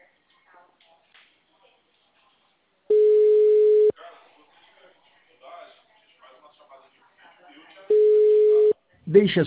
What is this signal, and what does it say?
Ringback tone (Europe, 425 Hz), with speech.